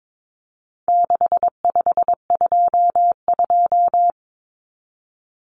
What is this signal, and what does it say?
Morse code, heard as a keyed tone, about 22 wpm, 700 Hz. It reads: 6522
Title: Common Cut Numbers - 4-Digit 22wpm